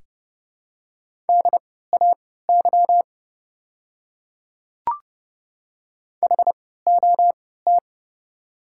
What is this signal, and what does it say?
Morse code, heard as a keyed tone, about 30 wpm, spaced as for 17 wpm, 700 Hz. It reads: DAY HOT
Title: 3x Character Spacing - Top 100 Words - 30wpm